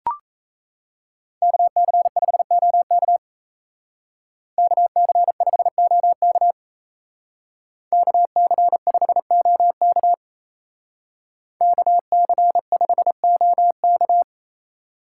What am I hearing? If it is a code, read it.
Morse code, heard as a keyed tone, about 36 wpm, 700 Hz. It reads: KC5OK KC5OK KC5OK KC5OK